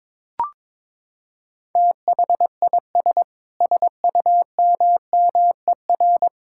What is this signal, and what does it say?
Morse code, heard as a keyed tone, about 22 wpm, 700 Hz. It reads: THIS SUMMER
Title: Sentences from Top 1600 Words 22wpm